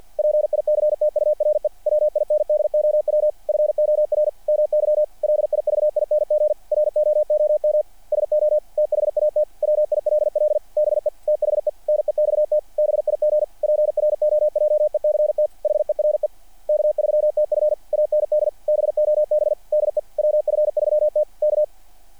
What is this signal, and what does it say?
Morse code, heard as a keyed tone, about 48 wpm, 600 Hz. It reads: PICTURE WINDOW FOR MY LIVING ROOM SO THAT WILL BE THE NEXT BIG PROJECT HERE K3TF AND BOB DE WF3T K
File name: archived / CRSnet